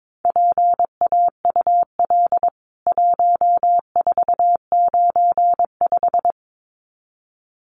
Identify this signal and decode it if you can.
Morse code, heard as a keyed tone, about 22 wpm, 700 Hz. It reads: PAUL 1495
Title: CWT Exchange 22wpm